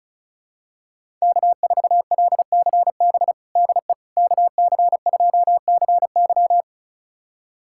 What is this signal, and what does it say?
Morse code, heard as a keyed tone, about 34 wpm, 700 Hz. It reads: K4LCB DE KC2CY
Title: USA Callsign DE Callsign - 5 Characters - 35wpm